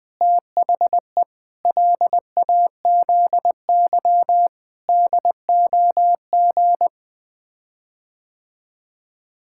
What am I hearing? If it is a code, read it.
Morse code, heard as a keyed tone, about 21 wpm, 700 Hz. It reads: THE LAZY DOG
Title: Sets of 8 Words - 20wpm